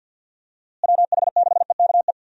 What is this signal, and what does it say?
Morse code, heard as a keyed tone, about 52 wpm, 700 Hz. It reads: WH6EKE
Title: USA Call Signs - 6 Characters - Repeated 2x - 50wpm